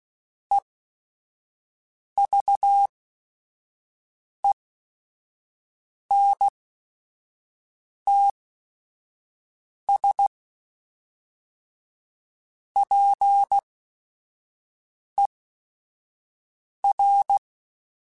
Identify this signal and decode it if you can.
Morse code, heard as a keyed tone, about 16 wpm, spaced as for 5 wpm, 800 Hz. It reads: EVENTS PER
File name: Technical Subjects / Technical Subjects-mp3s / XC @5